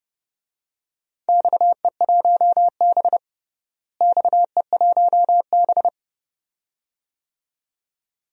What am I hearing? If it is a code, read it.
Morse code, heard as a keyed tone, about 30 wpm, 700 Hz. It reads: XE1B XE1B
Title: DXCC-100 Call Signs - 4 Characters - Repeated 2x - 30wpm